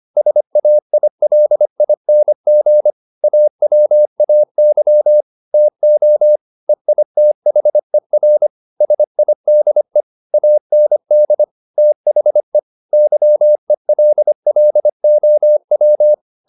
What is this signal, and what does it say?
Morse code, heard as a keyed tone, about 25 wpm, 600 Hz. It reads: SAILING AWAY TO EITHER SIDE AND THE YELLOW